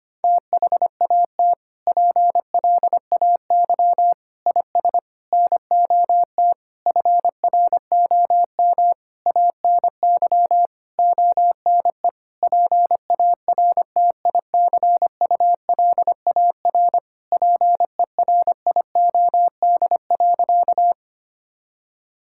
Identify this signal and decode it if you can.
Morse code, heard as a keyed tone, about 25 wpm, 700 Hz. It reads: THAT PLAY IS NOT FROM ANY ONE PARTICULAR PERIOD.
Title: Sentences from Top 1000 Words - Review Part 5 - 25wpm